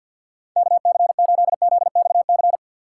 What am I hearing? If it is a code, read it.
Morse code, heard as a keyed tone, about 53 wpm, 700 Hz. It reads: KC8ZXC
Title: USA Call Signs - 6 Characters - 4x Character Spacing - 50wpm